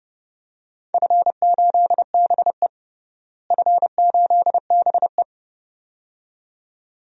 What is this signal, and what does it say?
Morse code, heard as a keyed tone, about 30 wpm, 700 Hz. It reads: F8BE F8BE
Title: DXCC-100 Call Signs - 4 Characters - Repeated 2x - 30wpm